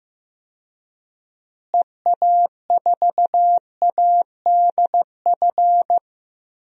Morse code, heard as a keyed tone, about 15 wpm, 700 Hz. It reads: EA4ADF
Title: DXCC-100 Call Signs - 6 Characters - Repeated 2x - 15wpm